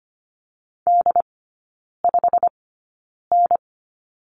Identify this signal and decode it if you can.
Morse code, heard as a keyed tone, about 25 wpm, spaced as for 9 wpm, 700 Hz. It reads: D5N
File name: course / 25:10wpm / 093 Lesson 16e - Letter c - Three Characters 25wpm